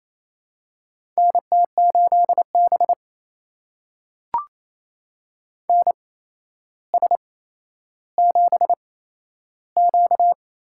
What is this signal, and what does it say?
Morse code, heard as a keyed tone, about 28 wpm, spaced as for 12 wpm, 700 Hz. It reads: NT8B NS7Q
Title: USA Call Signs - 4 Characters - 8x Character Spacing - 28wpm